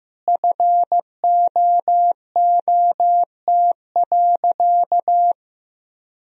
Morse code, heard as a keyed tone, about 15 wpm, 700 Hz. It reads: FOOT.
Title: Sentences from Top 1000 Words - Review Part 3 - 15wpm